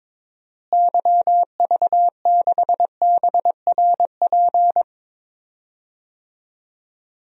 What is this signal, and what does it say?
Morse code, heard as a keyed tone, about 22 wpm, 700 Hz. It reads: YV6BRP
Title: DXCC-100 Call Signs - 6 Characters - Repeated 2x - 22wpm